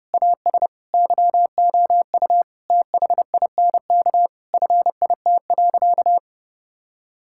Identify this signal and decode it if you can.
Morse code, heard as a keyed tone, about 30 wpm, 700 Hz. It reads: AS YOU THINK FIT.
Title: Sentences from Top 1000 Words - Review Part 5 - 30wpm